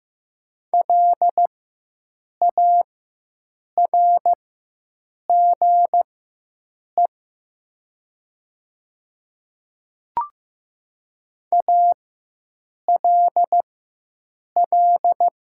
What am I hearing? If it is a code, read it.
Morse code, heard as a keyed tone, about 15 wpm, spaced as for 7 wpm, 700 Hz. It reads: LARGE ALL
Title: 4x Character Spacing - Top 200 Words - 15wpm